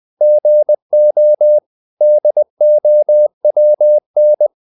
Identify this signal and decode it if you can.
Morse code, heard as a keyed tone, about 20 wpm, 600 Hz. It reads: GO DOWN
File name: the old man and the sea part 2 20wpm